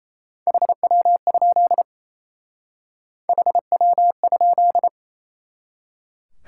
Morse code, heard as a keyed tone, about 35 wpm, 700 Hz. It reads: HW? HW?
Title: Speed-Racing Top 90 QSO Elements 28wpm